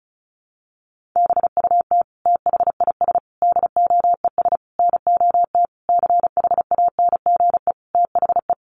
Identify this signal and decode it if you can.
Morse code, heard as a keyed tone, about 34 wpm, 700 Hz. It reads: BUT THIS DOES NOT CHANGE THE
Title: Sentences from Top 1400 Words 35wpm